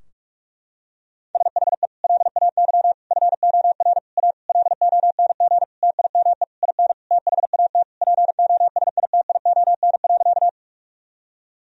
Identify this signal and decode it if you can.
Morse code, heard as a keyed tone, about 46 wpm, 700 Hz. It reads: SHE LAY FOR A LONG TIME IN THAT POSITION.
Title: Sentences from Top 1000 Words - Review Part 2 - 45wpm